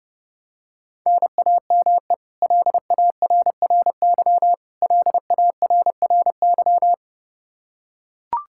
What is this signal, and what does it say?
Morse code, heard as a keyed tone, about 30 wpm, 700 Hz. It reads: NAME LARRY LARRY
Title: Practice QSO - Exchange 1 - Beginning - 30wpm